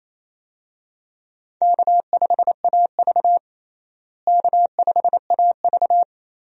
Morse code, heard as a keyed tone, about 28 wpm, 700 Hz. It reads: K5AV K5AV
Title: USA Call Signs - 4 Characters - Repeated 3x - 28wpm